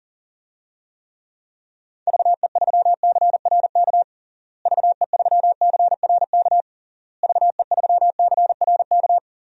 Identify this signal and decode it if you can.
Morse code, heard as a keyed tone, about 37 wpm, 700 Hz. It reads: VE3CRK VE3CRK VE3CRK
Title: DXCC-100 Call Signs - 6 Characters - Repeated 3x - 40wpm